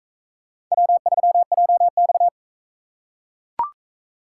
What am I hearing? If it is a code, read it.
Morse code, heard as a keyed tone, about 38 wpm, 700 Hz. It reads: W3JX